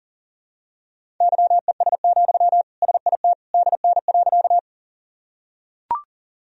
Morse code, heard as a keyed tone, about 40 wpm, 700 Hz. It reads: YES, SIT DN.
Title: Sentences from Top 500 Words and Abbreviations 40wpm